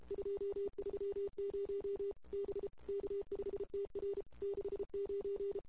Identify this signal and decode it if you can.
Morse code, heard as a keyed tone, about 32 wpm, 410 Hz. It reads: 230 B K5TR 69
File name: CW / 2008-ss-cw-k5tr-op-n5rz -23:00